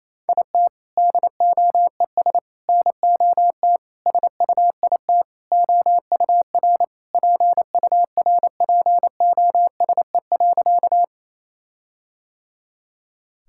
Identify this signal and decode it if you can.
Morse code, heard as a keyed tone, about 28 wpm, 700 Hz. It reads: IT DOES NOT SUIT OUR PURPOSE.
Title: Sentences from Top 1600 Words 28wpm